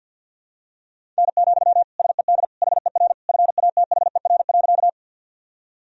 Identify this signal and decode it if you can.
Morse code, heard as a keyed tone, about 53 wpm, 700 Hz. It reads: N, SED HER FATHER.
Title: Sentences from Top 300 Words and Abbreviations 50wpm